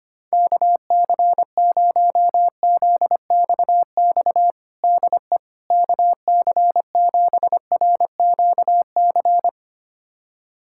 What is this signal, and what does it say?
Morse code, heard as a keyed tone, about 25 wpm, 700 Hz. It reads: KC0ZXX DE KC7RQC
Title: USA Callsign DE Callsign 25wpm